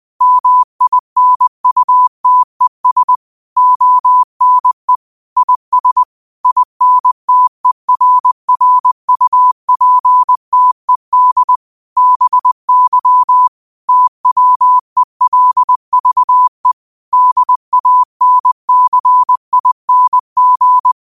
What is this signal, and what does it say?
Morse code, heard as a keyed tone, about 20 wpm, 1000 Hz. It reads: MINUTES ONE IS INTERRUPTED BY TWELVE DANCING